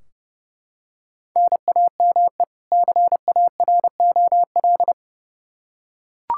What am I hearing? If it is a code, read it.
Morse code, heard as a keyed tone, about 30 wpm, 700 Hz. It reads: NAME CAROL
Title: K1USN SST QSO Practice 30wpm